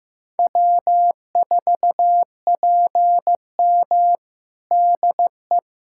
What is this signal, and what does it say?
Morse code, heard as a keyed tone, about 15 wpm, 700 Hz. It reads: W4PM DE
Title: Practice QSO - Exchange 4 - Ending - 15wpm